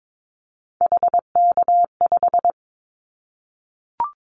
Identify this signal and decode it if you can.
Morse code, heard as a keyed tone, about 22 wpm, 700 Hz. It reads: HK5
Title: Callsign Prefixes for Top 100 DXCC - 22wpm